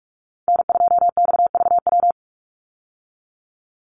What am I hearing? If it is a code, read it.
Morse code, heard as a keyed tone, about 44 wpm, 700 Hz. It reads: N2XVW
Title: USA Call Signs 45wpm